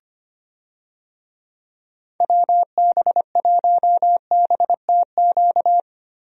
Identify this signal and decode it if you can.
Morse code, heard as a keyed tone, about 25 wpm, 700 Hz. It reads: WB1BTQ
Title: USA Call Signs - 6 Characters - Repeated 3x - 25wpm